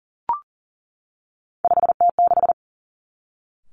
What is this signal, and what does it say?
Morse code, heard as a keyed tone, about 40 wpm, 700 Hz. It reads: 5T6
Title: Three Challenging Letter-Number Combinations 40wpm